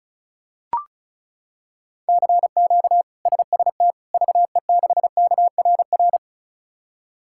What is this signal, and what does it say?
Morse code, heard as a keyed tone, about 34 wpm, 700 Hz. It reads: CQ SST VE6KRR
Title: K1USN SST QSO Practice - 35wpm